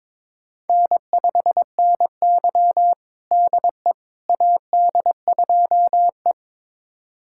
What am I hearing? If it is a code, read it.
Morse code, heard as a keyed tone, about 22 wpm, 700 Hz. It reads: N5NY DE AD2E
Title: USA Callsign DE Callsign - 4 Characters - 22wpm